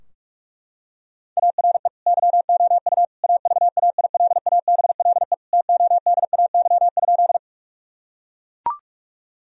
Morse code, heard as a keyed tone, about 41 wpm, 700 Hz. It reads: ARE YOU AVAILABLE TODAY?
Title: Sentences from Top 1000 Words - Review Part 3 - 45wpm